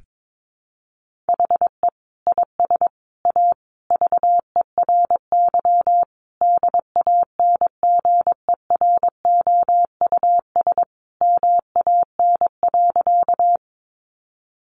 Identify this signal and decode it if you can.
Morse code, heard as a keyed tone, about 22 wpm, 700 Hz. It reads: HE IS A VERY DANGEROUS MAN.
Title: Sentences from Top 1200 Words 22wpm